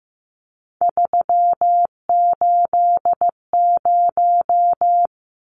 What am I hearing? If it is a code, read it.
Morse code, heard as a keyed tone, about 15 wpm, 700 Hz. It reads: 380
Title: Numbers - 3-Digit 15wpm